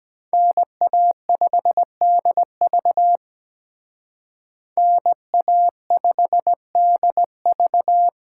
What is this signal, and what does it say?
Morse code, heard as a keyed tone, about 20 wpm, 700 Hz. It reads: NA5DV NA5DV
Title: Speed Racing - USA 2x2 Call Signs 17wpm